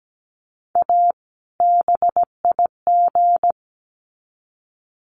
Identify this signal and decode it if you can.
Morse code, heard as a keyed tone, about 17 wpm, 700 Hz. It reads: A BIG
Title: Sets of 2 Words 17wpm